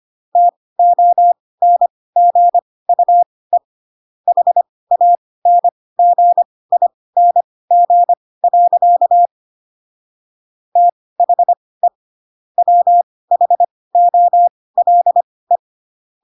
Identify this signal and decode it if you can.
Morse code, heard as a keyed tone, about 25 wpm, spaced as for 18 wpm, 700 Hz. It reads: TONGUE HANGING. THE WHOLE
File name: Pinocchio / Pinocchio ch6 12 25 ik1hge